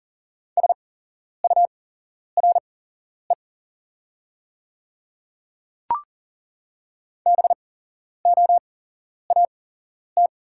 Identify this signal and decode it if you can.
Morse code, heard as a keyed tone, about 40 wpm, spaced as for 11 wpm, 700 Hz. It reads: SURE BOAT
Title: 8x Character Spacing - Top 500 Words - 40wpm